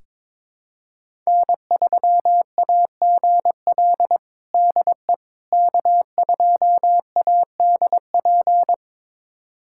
Morse code, heard as a keyed tone, about 22 wpm, 700 Hz. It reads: N3AGL DE K2ADP
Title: USA Callsign DE Callsign - 5 Characters - 22wpm